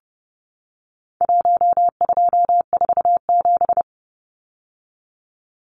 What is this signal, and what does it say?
Morse code, heard as a keyed tone, about 30 wpm, 700 Hz. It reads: 1247
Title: Numbers - Contest Style - 4-Digit 30wpm